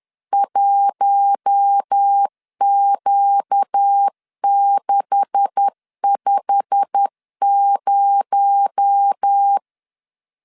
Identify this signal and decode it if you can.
Morse code, heard as a keyed tone, about 10 wpm, 800 Hz. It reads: 1Q650
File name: LEZIONI / 10 SIGNS / 10B INT 10wpm 2844